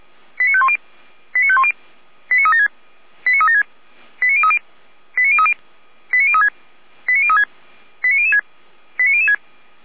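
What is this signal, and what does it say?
Five-tone selective call (ZVEI), heard as quick runs of five tones, tones of about 70 ms. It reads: 89510 89510 89376 89376 89030 89030 89036 89036 89006 89006
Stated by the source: ZVEI five-tone call 89014 received on SDR.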